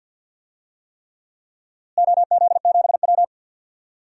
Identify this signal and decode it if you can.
Morse code, heard as a keyed tone, about 52 wpm, 700 Hz. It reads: OZ7W
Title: DXCC-100 Call Signs - 4 Characters - Repeated 3x - 50wpm